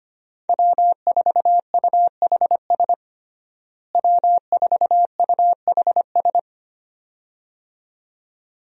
Morse code, heard as a keyed tone, about 25 wpm, 700 Hz. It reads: W4UHS W4UHS
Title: USA Call Signs - 5 Characters - Repeated 2x - 25wpm